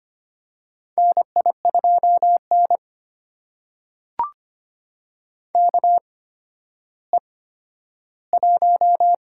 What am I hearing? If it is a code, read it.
Morse code, heard as a keyed tone, about 25 wpm, spaced as for 12 wpm, 700 Hz. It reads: NI2N KE1
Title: USA Call Signs - 4 Characters - 8x Character Spacing - 25wpm